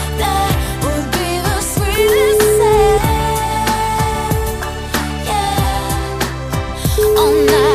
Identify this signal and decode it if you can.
Ringback tone (Europe, 425 Hz), with noise.